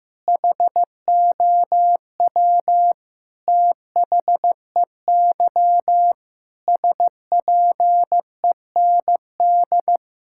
Morse code, heard as a keyed tone, about 15 wpm, 700 Hz. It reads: HOW THEY SPEND
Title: Sentences from Top 1000 Words - Review Part 5 - 15wpm